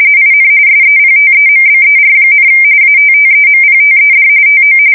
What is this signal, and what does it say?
RTTY, about 45.45 baud. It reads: IN SEPTEMBER, ATTEMPTED TO
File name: fsk170 / battleshipnj